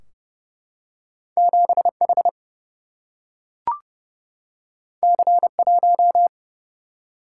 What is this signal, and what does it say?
Morse code, heard as a keyed tone, about 28 wpm, 700 Hz. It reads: 7 H C 1
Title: Single-Letter Single-Number 30wpm